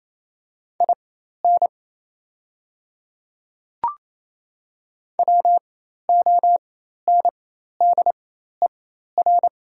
Morse code, heard as a keyed tone, about 27 wpm, spaced as for 13 wpm, 700 Hz. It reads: IN WONDER
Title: 4x Character Spacing - Top 500 Words - 28wpm